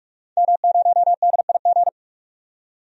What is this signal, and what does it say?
Morse code, heard as a keyed tone, about 46 wpm, 700 Hz. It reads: M0DIG
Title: DXCC-100 Call Signs - 5 Characters - 45wpm